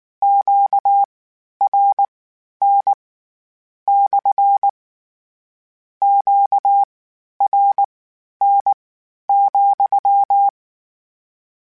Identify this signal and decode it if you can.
Morse code, heard as a keyed tone, about 19 wpm, spaced as for 11 wpm, 800 Hz. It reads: QRN / QRN,
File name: Q Calls / Q Calls-mp3s / Q CALLS SPSL @18